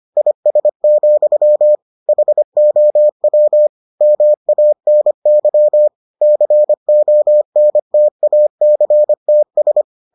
Morse code, heard as a keyed tone, about 25 wpm, 600 Hz. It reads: IS, HOW MANY CONTACTS